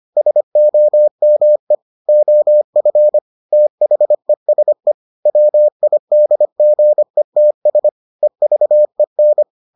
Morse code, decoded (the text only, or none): SOME OF THESE WIDGETS EVEN